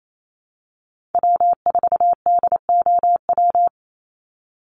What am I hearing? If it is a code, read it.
Morse code, heard as a keyed tone, about 28 wpm, 700 Hz. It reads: W4DOW
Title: NA SOTA Chaser Callsigns 28wpm